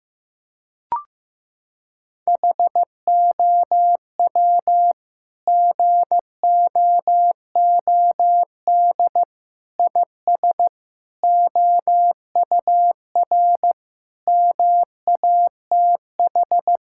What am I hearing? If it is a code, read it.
Morse code, heard as a keyed tone, about 15 wpm, 700 Hz. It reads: HOW GOOD IS OUR MATH
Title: Sentences from Top 1000 Words - Review Part 6 - 15wpm